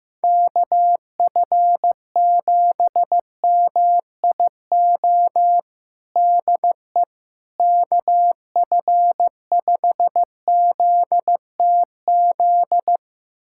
Morse code, heard as a keyed tone, about 15 wpm, 700 Hz. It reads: KF7MIO DE KF5ZTZ
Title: USA Callsign DE Callsign 15wpm